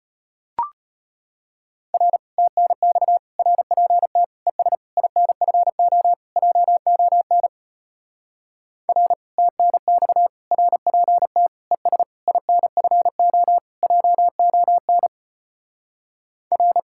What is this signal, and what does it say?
Morse code, heard as a keyed tone, about 33 wpm, 700 Hz. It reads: R TNX RPT ES INFO JON R TNX RPT ES INFO JON R